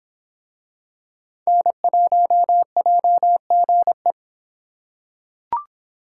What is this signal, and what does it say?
Morse code, heard as a keyed tone, about 26 wpm, 700 Hz. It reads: N1JGE